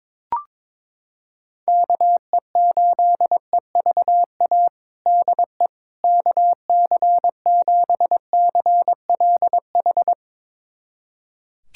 Morse code, heard as a keyed tone, about 22 wpm, 700 Hz. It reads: KE8EVA DE KC7CLH
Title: USA Callsign DE Callsign 22wpm